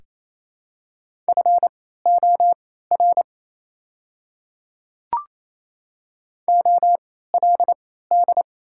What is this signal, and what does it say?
Morse code, heard as a keyed tone, about 28 wpm, spaced as for 16 wpm, 700 Hz. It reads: FOR OLD
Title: 3x Character Spacing - Top 200 Words - 28wpm